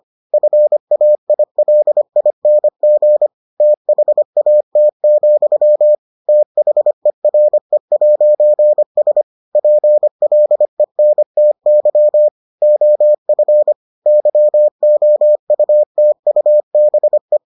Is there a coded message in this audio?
Yes: Morse code from 0.3 to 17.4 s, heard as a keyed tone, about 25 wpm, 600 Hz. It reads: FAILING THAT, THERE'S PLENTY OF YOUTUBE